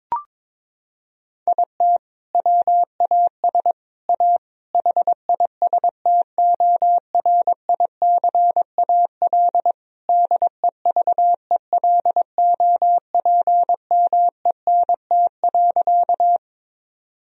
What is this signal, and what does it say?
Morse code, heard as a keyed tone, about 23 wpm, 700 Hz. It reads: IT WAS A HISTORICAL DEVELOPMENT.
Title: Sentences from Top 1000 Words - Review Part 4 - 22wpm